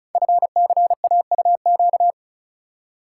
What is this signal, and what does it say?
Morse code, heard as a keyed tone, about 36 wpm, 700 Hz. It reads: FCAUQ
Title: Code Groups - 5 Letters - 35wpm